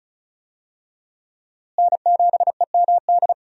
Morse code, heard as a keyed tone, about 36 wpm, 700 Hz. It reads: N7EMD
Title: USA Call Signs - 5 Characters - Repeated 3x - 35wpm